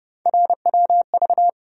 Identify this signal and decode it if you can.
Morse code, heard as a keyed tone, about 30 wpm, 700 Hz. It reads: RWV
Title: USA Call Sign Postfixes - 3 Letters 30wpm